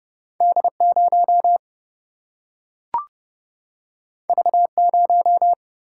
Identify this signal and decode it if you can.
Morse code, heard as a keyed tone, about 30 wpm, 700 Hz. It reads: D0 V0
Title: Single-Letter Single-Number 30wpm